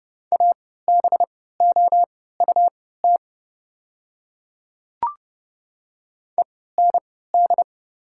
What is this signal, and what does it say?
Morse code, heard as a keyed tone, about 30 wpm, spaced as for 17 wpm, 690 Hz. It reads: ABOUT END
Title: 3x Character Spacing - Top 500 Words - 30wpm